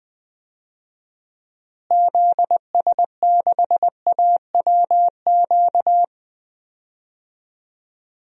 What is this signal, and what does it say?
Morse code, heard as a keyed tone, about 19 wpm, 700 Hz. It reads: ZS6AWQ
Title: DXCC-100 Call Signs - 6 Characters - Repeated 3x - 20wpm